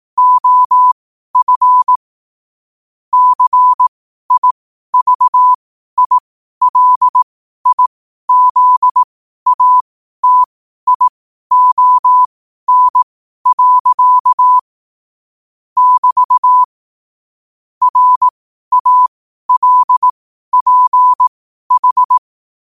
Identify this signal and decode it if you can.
Morse code, heard as a keyed tone, about 18 wpm, spaced as for 12 wpm, 1000 Hz. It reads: OF CIVILIZATION. <BT> RALPH
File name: qotdfiles / 12wpm